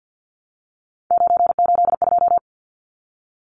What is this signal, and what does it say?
Morse code, heard as a keyed tone, about 50 wpm, 700 Hz. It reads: 982
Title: Numbers - 3-Digit 50wpm